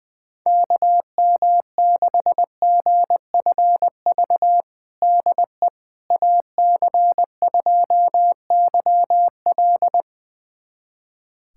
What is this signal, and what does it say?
Morse code, heard as a keyed tone, about 20 wpm, 700 Hz. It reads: KM6GFV DE AC2YL
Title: USA Callsign DE Callsign 20wpm